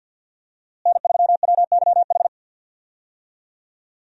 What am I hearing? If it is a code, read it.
Morse code, heard as a keyed tone, about 52 wpm, 690 Hz. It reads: N3WYH
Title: USA Call Signs - 5 Characters - Repeated 3x - 50wpm